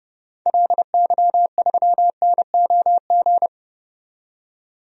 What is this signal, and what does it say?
Morse code, heard as a keyed tone, about 30 wpm, 700 Hz. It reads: LY3NOG
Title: DXCC-100 Call Signs - 6 Characters - 30wpm